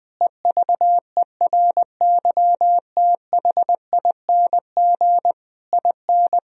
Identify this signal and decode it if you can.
Morse code, heard as a keyed tone, about 20 wpm, 700 Hz. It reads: EVERYTHING IN
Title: Sentences from Top 1000 Words - Review Part 2 - 20wpm